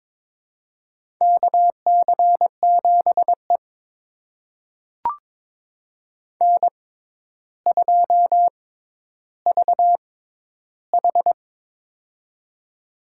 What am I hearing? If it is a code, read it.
Morse code, heard as a keyed tone, about 23 wpm, spaced as for 11 wpm, 700 Hz. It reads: KC7E N2VH
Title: USA Call Signs - 4 Characters - 6x Character Spacing - 22wpm